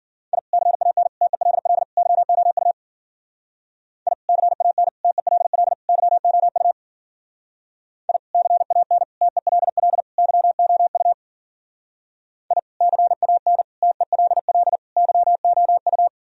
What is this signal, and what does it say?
Morse code, heard as a keyed tone, about 48 wpm, 700 Hz. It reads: I CAN TELL YOU I CAN TELL YOU I CAN TELL YOU I CAN TELL YOU
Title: Speed-Racing - Sets of 4 Words 40wpm